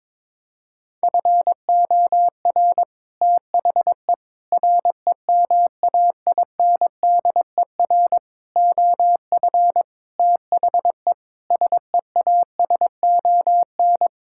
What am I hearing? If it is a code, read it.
Morse code, heard as a keyed tone, about 21 wpm, 700 Hz. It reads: FOR THE REMAINDER OF THE SEASON
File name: practice / 22wpm / Sets of 6 Words - 22wpm